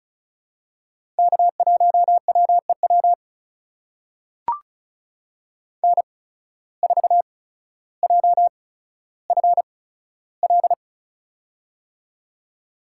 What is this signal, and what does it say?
Morse code, heard as a keyed tone, about 36 wpm, spaced as for 15 wpm, 700 Hz. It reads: K1WEW N4JFL
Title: USA Call Signs - 5 Characters - 8x Character Spacing - 35wpm